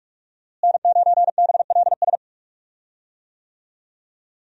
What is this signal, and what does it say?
Morse code, heard as a keyed tone, about 43 wpm, 700 Hz. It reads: N9BLS
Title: USA Call Signs - 5 Characters - Repeated 2x - 45wpm